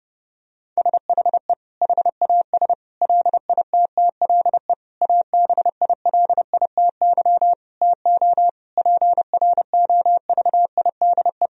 Morse code, heard as a keyed tone, about 29 wpm, 700 Hz. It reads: SHE HAS LITTLE ABILITY TO PROVIDE
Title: Sentences from Top 1000 Words - Review Part 3 - 30wpm